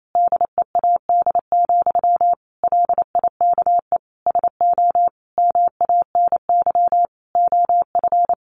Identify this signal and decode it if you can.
Morse code, heard as a keyed tone, about 28 wpm, 700 Hz. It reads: DEAD, LIKE SO MANY OF
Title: Sentences from Top 1500 Words 28wpm